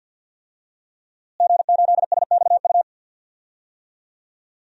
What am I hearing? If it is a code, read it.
Morse code, heard as a keyed tone, about 52 wpm, 700 Hz. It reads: G8SXU
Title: DXCC-100 Call Signs - 5 Characters - Repeated 2x - 50wpm